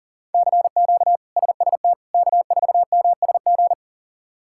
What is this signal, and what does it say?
Morse code, heard as a keyed tone, about 40 wpm, 690 Hz. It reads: CQ SST K4MSG
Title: K1USN SST QSO Practice - 40wpm